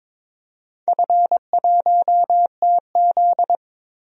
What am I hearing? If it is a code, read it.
Morse code, heard as a keyed tone, about 23 wpm, 700 Hz. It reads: F1TZ
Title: DXCC-100 Call Signs - 4 Characters - 22wpm